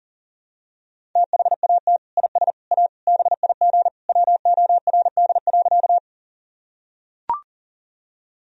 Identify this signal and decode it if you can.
Morse code, heard as a keyed tone, about 40 wpm, 700 Hz. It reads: THAT IS A BIG WORD.
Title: Sentences from Top 1000 Words - Review Part 1 - 40wpm